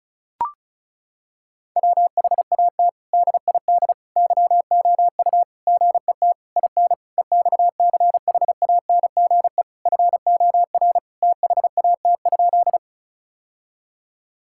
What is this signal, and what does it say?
Morse code, heard as a keyed tone, about 36 wpm, 700 Hz. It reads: WHAT DID YOU GET IN EXCHANGE FOR THAT?
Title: Sentences from Top 1000 Words - Review Part 6 - 35wpm